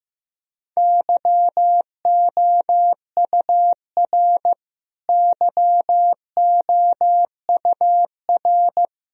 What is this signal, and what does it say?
Morse code, heard as a keyed tone, about 16 wpm, 700 Hz. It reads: YOUR YOUR
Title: Top 100 Words Repeated 15wpm